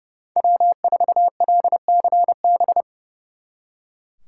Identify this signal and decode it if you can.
Morse code, heard as a keyed tone, about 36 wpm, 700 Hz. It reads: W4LCB
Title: Speed Racing - USA 1x3 Call Signs 30wpm